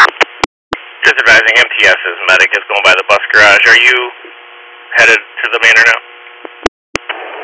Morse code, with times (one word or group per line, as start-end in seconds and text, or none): none